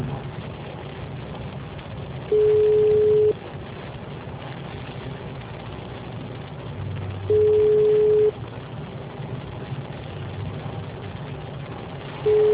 Ringback tone (Europe, 425 Hz).